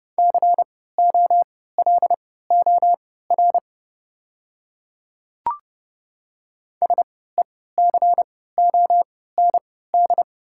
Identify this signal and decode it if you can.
Morse code, heard as a keyed tone, about 30 wpm, spaced as for 17 wpm, 690 Hz. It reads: COLOR SECOND
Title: 3x Character Spacing - Top 300 Words - 30wpm